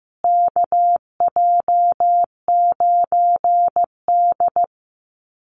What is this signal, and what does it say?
Morse code, heard as a keyed tone, about 15 wpm, 700 Hz. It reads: KJ9D
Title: USA Call Signs Repeated 3x 15wpm